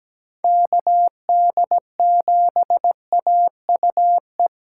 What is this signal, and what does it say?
Morse code, heard as a keyed tone, about 17 wpm, 700 Hz. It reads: KD7AUE
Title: USA Callsign DE Callsign - 6 Characters - 17wpm